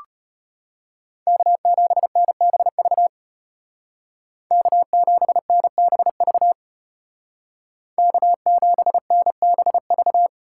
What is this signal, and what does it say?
Morse code, heard as a keyed tone, about 32 wpm, 700 Hz. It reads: K7NBV K7NBV K7NBV